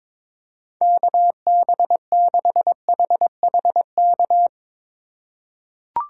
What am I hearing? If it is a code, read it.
Morse code, heard as a keyed tone, about 22 wpm, 700 Hz. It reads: KB6HHK
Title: USA Call Signs - 6 Characters - 2x Character Spacing - 22wpm